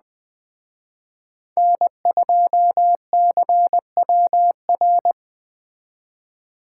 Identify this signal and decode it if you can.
Morse code, heard as a keyed tone, about 20 wpm, 700 Hz. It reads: N2CWR
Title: Speed Racing - USA 1x3 Call Signs 17wpm